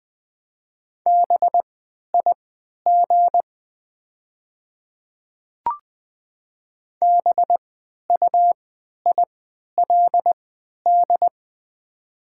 Morse code, heard as a keyed tone, about 20 wpm, spaced as for 11 wpm, 700 Hz. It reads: BIG BUILD
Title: 3x Character Spacing - Top 500 Words - 20wpm